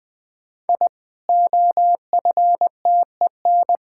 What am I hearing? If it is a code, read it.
Morse code, heard as a keyed tone, about 20 wpm, 700 Hz. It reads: I OFTEN
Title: Sentences with 8-Word Phrases - Spoken Beginning - 20wpm